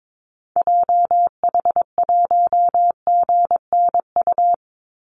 Morse code, heard as a keyed tone, about 22 wpm, 690 Hz. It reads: JH1GNU
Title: Call Signs Repeated 2x - 22wpm